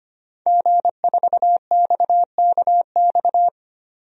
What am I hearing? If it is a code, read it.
Morse code, heard as a keyed tone, about 25 wpm, 700 Hz. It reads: G4XKX
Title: DXCC-100 Call Signs - 5 Characters - 25wpm